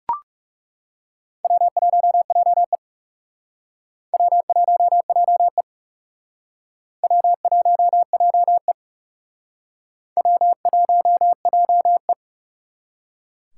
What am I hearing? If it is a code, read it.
Morse code, heard as a keyed tone, about 40 wpm, 700 Hz. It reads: W1JE W1JE W1JE W1JE